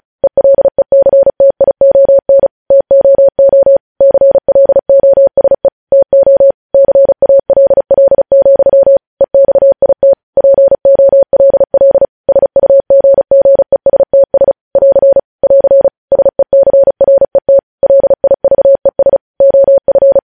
Morse code, heard as a keyed tone, about 36 wpm, 550 Hz. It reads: ELECTION TOO CLOSE TO CALL, EXIT POLL SUGGESTS <AR> <AR> SECRET LIVES OF
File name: podcast / News headlines-1x-35x35